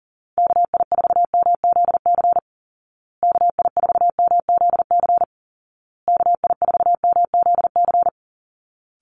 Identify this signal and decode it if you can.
Morse code, heard as a keyed tone, about 40 wpm, 700 Hz. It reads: KI4MZC KI4MZC KI4MZC
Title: Call Signs Repeated 2x - 40wpm